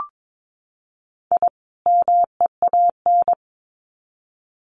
Morse code, heard as a keyed tone, about 22 wpm, 700 Hz. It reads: I MEAN
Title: Sets of 2 Words 22wpm